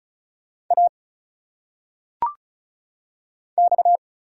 Morse code, heard as a keyed tone, about 34 wpm, 700 Hz. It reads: A X
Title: Single Letters - Phonetic Pronunciation 35wpm